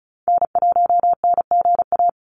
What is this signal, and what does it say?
Morse code, heard as a keyed tone, about 34 wpm, 700 Hz. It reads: N1NGA
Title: USA Call Signs 35wpm